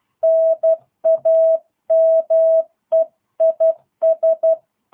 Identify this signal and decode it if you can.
Morse code, heard as a keyed tone, about 12 wpm, 650 Hz. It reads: NAME IS
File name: jh8cbh / 2022cw